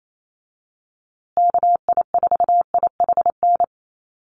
Morse code, heard as a keyed tone, about 28 wpm, 700 Hz. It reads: KI4IHN
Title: USA Call Signs Repeated 3x 28wpm